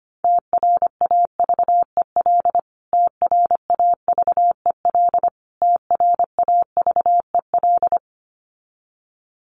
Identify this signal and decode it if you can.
Morse code, heard as a keyed tone, about 25 wpm, 700 Hz. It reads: TRAVEL TRAVEL TRAVEL
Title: Top 500 Words Repeated 25wpm